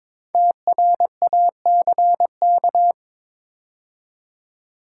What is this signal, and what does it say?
Morse code, heard as a keyed tone, about 22 wpm, 700 Hz. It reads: TRACK
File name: 5 Letter Words - First 3 Letters Given - 22wpm